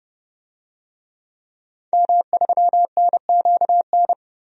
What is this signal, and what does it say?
Morse code, heard as a keyed tone, about 30 wpm, 700 Hz. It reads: M3NQN
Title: DXCC-100 Call Signs - 5 Characters - Repeated 2x - 30wpm